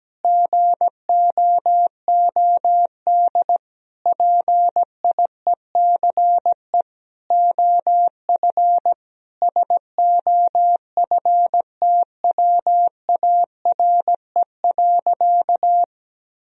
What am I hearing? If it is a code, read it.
Morse code, heard as a keyed tone, about 17 wpm, 700 Hz. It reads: GOOD PIECE OF SOFTWARE.
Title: Sentences from Top 1000 Words - Review Part 3 - 17wpm